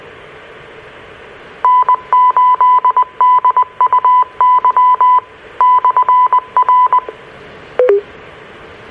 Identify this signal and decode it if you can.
Morse code, heard as a keyed tone, about 20 wpm, 1000 Hz. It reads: N8DUY/R
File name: RR Skywarn / Washtenaw County SKYWARN N8DUY and W8SRC Repeaters